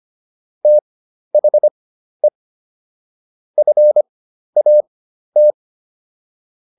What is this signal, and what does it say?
Morse code, heard as a keyed tone, about 25 wpm, spaced as for 12 wpm, 600 Hz. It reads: THE FAT